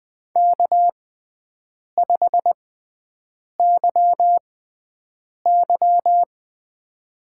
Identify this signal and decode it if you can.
Morse code, heard as a keyed tone, about 21 wpm, spaced as for 7 wpm, 700 Hz. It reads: K5YY
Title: USA Call Signs - 4 Characters - 6x Character Spacing - 20wpm